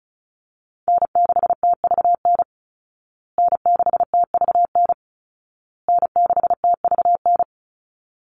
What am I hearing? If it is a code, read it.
Morse code, heard as a keyed tone, about 34 wpm, 700 Hz. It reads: N6TVN N6TVN N6TVN
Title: Call Signs Repeated 2x - 35wpm